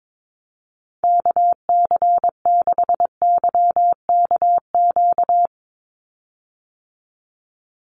Morse code, heard as a keyed tone, about 22 wpm, 700 Hz. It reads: KC6YKQ
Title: USA Call Signs 22wpm